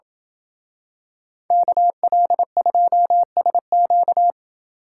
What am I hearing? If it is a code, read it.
Morse code, heard as a keyed tone, about 26 wpm, 700 Hz. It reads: KL2SQ